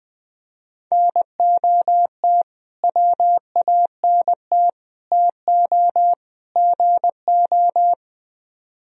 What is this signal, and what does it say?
Morse code, heard as a keyed tone, about 20 wpm, 700 Hz. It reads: NOT WANT TO GO
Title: Speed-Racing - Sets of 4 Words 17wpm